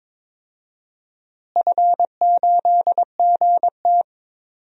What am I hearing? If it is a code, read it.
Morse code, heard as a keyed tone, about 22 wpm, 700 Hz. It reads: F8GT
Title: DXCC-100 Call Signs - 4 Characters - Repeated 3x - 22wpm